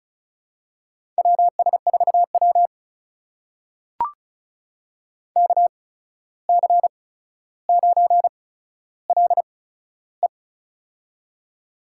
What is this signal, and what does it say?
Morse code, heard as a keyed tone, about 36 wpm, spaced as for 14 wpm, 700 Hz. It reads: WS4W KC9LE
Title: USA Call Signs - 8x Character Spacing - 35wpm